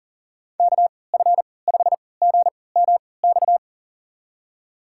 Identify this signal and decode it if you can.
Morse code, heard as a keyed tone, about 37 wpm, spaced as for 22 wpm, 700 Hz. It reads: KF5GMX
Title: USA Call Signs - 3x Character Spacing - 40wpm